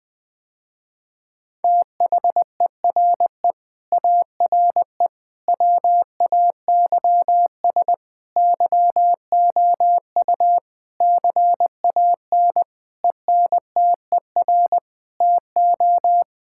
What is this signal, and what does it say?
Morse code, heard as a keyed tone, about 21 wpm, 700 Hz. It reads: THERE ARE WAYS YOU CAN ENTER TO
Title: Sentences with 8-Word Phrases - Spoken Beginning - 20wpm